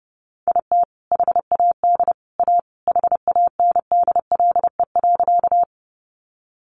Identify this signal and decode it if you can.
Morse code, heard as a keyed tone, about 30 wpm, 700 Hz. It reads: IT HAD A HANDLE.
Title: Sentences from Top 1100 Words 30wpm